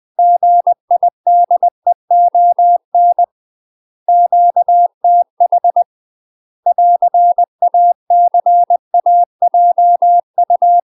Morse code, heard as a keyed tone, about 20 wpm, 700 Hz. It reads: GIDEON QTH <AR>ACAJU
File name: UK MASTER QSOs TRAINING / 20WPM / UK MASTER QSOs 0037